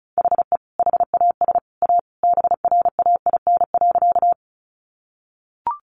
Morse code, heard as a keyed tone, about 34 wpm, 700 Hz. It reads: HE HAS A BRAIN.
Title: Sentences from Top 1100 Words 35wpm